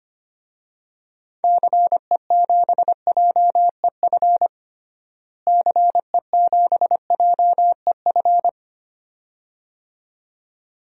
Morse code, heard as a keyed tone, about 25 wpm, 700 Hz. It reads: CE7JEF CE7JEF
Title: DXCC-100 Call Signs - 6 Characters - Repeated 2x - 25wpm